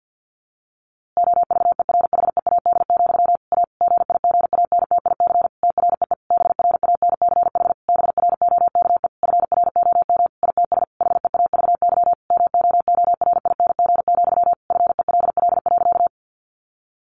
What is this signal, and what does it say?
Morse code, heard as a keyed tone, about 48 wpm, 700 Hz. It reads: OVERHEAD, A GIGANTIC TREE BRANCH BROKE FROM ITS HEAVY MOORING, FELL.
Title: A Sound Of Thunder 50wpm